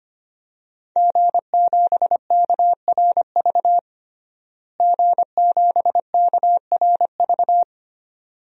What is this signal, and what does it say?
Morse code, heard as a keyed tone, about 25 wpm, 700 Hz. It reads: G7KRV G7KRV
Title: DXCC-100 Call Signs - 5 Characters - Repeated 2x - 25wpm